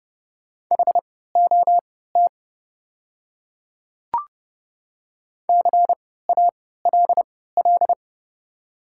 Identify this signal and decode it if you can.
Morse code, heard as a keyed tone, about 31 wpm, spaced as for 17 wpm, 700 Hz. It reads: HOT CALL
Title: 3x Character Spacing - Top 100 Words - 30wpm